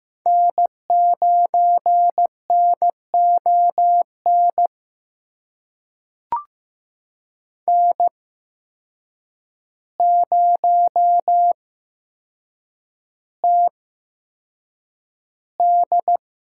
Morse code, heard as a keyed tone, about 15 wpm, spaced as for 7 wpm, 700 Hz. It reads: N9NON N0TD
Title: USA Call Signs - 5 Characters - 8x Character Spacing - 15wpm